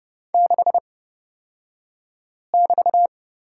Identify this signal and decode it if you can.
Morse code, heard as a keyed tone, about 29 wpm, spaced as for 5 wpm, 700 Hz. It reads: 6<BT>
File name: Lesson 36d - BT - Two Characters 30wpm